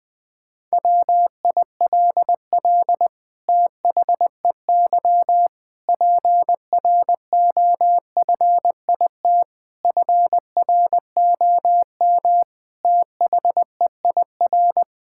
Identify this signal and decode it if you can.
Morse code, heard as a keyed tone, about 21 wpm, 700 Hz. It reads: WILL THEY PROFIT FROM THEIR
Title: Sentences from Top 1000 Words - Review Part 6 - 20wpm